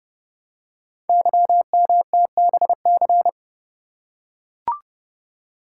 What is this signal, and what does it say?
Morse code, heard as a keyed tone, about 30 wpm, 690 Hz. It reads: YMTBC
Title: Code Groups - 5 Letters - 30wpm